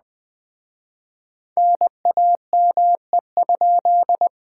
Morse code, heard as a keyed tone, about 21 wpm, 700 Hz. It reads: NAME?
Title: Speed-Racing Common QSO Phrases - 17wpm